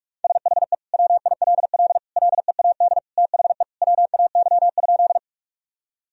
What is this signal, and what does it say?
Morse code, heard as a keyed tone, about 41 wpm, 700 Hz. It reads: SHE WILL LEAD THE WAY?
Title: Sentences from Top 1000 Words - Review Part 2 - 45wpm